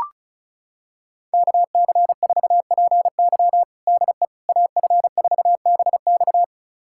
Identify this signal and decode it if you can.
Morse code, heard as a keyed tone, about 36 wpm, 700 Hz. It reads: KC4PY DE AF4BX
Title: USA Callsign DE Callsign - 5 Characters - 35wpm